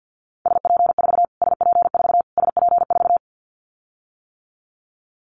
Morse code, heard as a keyed tone, about 48 wpm, 700 Hz. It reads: SP4 SP4 SP4
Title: Callsign Prefixes for Top 25 DXCC - Repeated 3x - 50wpm